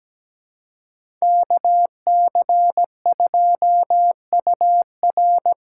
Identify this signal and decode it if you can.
Morse code, heard as a keyed tone, about 17 wpm, 700 Hz. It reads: KC2UR
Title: USA Call Signs - 6 Characters - 8x Character Spacing - 17wpm